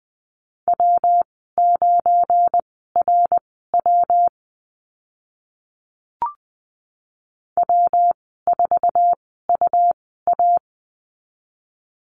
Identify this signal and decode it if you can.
Morse code, heard as a keyed tone, about 20 wpm, spaced as for 14 wpm, 700 Hz. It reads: W9RW W4UA
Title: 193 Lesson 31g - Letter v - Call Signs 20wpm